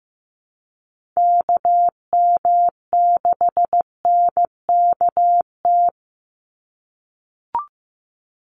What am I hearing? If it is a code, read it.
Morse code, heard as a keyed tone, about 15 wpm, 700 Hz. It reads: KM6NKT
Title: USA Call Signs 15wpm